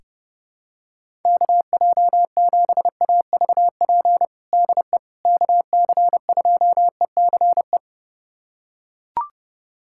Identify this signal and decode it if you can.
Morse code, heard as a keyed tone, about 30 wpm, 700 Hz. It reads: KJ7AVP DE KC2ECE
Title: USA Callsign DE Callsign 30wpm